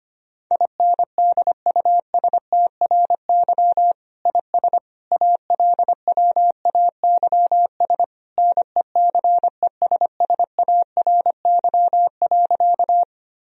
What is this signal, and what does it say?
Morse code, heard as a keyed tone, about 25 wpm, 700 Hz. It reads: INDUSTRY IS ALWAYS NECESSARY.
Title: Sentences from Top 1000 Words - Review Part 4 - 25wpm